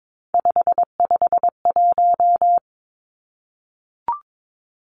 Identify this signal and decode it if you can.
Morse code, heard as a keyed tone, about 22 wpm, 700 Hz. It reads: 551
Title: Three Challenging Letter-Number Combinations 22wpm